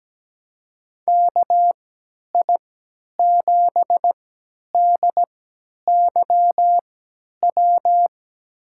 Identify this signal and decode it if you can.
Morse code, heard as a keyed tone, about 17 wpm, spaced as for 10 wpm, 700 Hz. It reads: KI7DYW
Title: USA Call Signs - 6 Characters - 3x Character Spacing - 17wpm